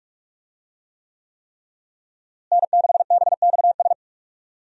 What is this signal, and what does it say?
Morse code, heard as a keyed tone, about 46 wpm, 700 Hz. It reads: N6BXS
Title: USA Call Signs - 5 Characters - Repeated 3x - 45wpm